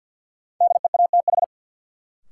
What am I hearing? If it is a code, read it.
Morse code, heard as a keyed tone, about 53 wpm, 700 Hz. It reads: DEATH
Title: Sentences from Top 1100 Words - New Word Drill - 50wpm